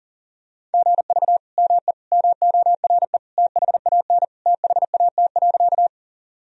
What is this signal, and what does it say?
Morse code, heard as a keyed tone, about 40 wpm, 700 Hz. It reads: GV ME MORE THAN THAT.
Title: Sentences from Top 200 Words and Abbreviations 40wpm